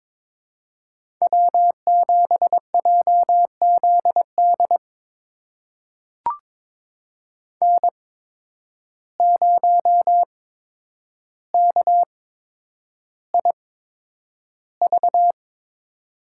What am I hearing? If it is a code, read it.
Morse code, heard as a keyed tone, about 22 wpm, spaced as for 9 wpm, 700 Hz. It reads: W7JZD N0KIV
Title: USA Call Signs - 5 Characters - 8x Character Spacing - 22wpm